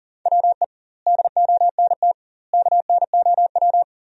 Morse code, heard as a keyed tone, about 40 wpm, 700 Hz. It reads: WE DONT KNOW
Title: 2x Word-Spacing - Sentences from Top 400 Words - 40wpm